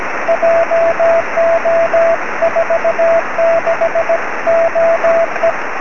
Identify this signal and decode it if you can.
Morse code, heard as a keyed tone, about 17 wpm, 670 Hz, with radio noise. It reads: JO46OE
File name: sm7lcb / 20-oz1ctz 23cm